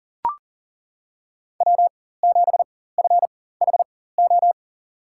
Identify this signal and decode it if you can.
Morse code, heard as a keyed tone, about 40 wpm, spaced as for 19 wpm, 700 Hz. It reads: W7FHO
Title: USA Call Signs - 4x Character Spacing - 40wpm